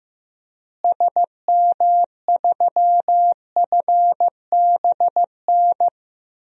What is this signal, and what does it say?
Morse code, heard as a keyed tone, about 15 wpm, 700 Hz. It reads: SM3FBN
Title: DXCC-100 Call Signs - 6 Characters - Repeated 2x - 15wpm